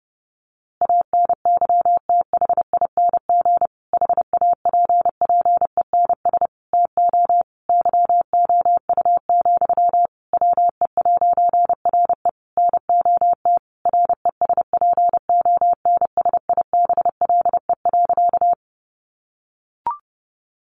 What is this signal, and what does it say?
Morse code, heard as a keyed tone, about 30 wpm, 700 Hz. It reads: ANYTHING HAPPENS TO YOU, WE'RE NOT RESPONSIBLE.
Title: A Sound Of Thunder 30wpm